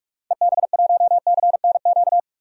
Morse code, heard as a keyed tone, about 46 wpm, 700 Hz. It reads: EB1CNQ
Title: DXCC-100 Call Signs - 6 Characters - 45wpm